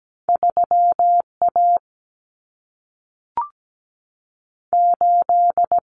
Morse code, heard as a keyed tone, about 17 wpm, 700 Hz. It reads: 3 A 8
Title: Common Cut Numbers - 2-Digit 17wpm